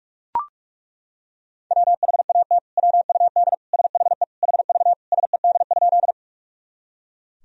Morse code, heard as a keyed tone, about 41 wpm, 700 Hz. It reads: WHAT WUD SHE HV SED?
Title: Sentences from Top 100 Words and Abbreviations 45wpm